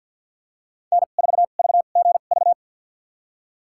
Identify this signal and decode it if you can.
Morse code, heard as a keyed tone, about 46 wpm, spaced as for 35 wpm, 700 Hz. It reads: N4VGV
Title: USA Call Signs - 2x Character Spacing - 50wpm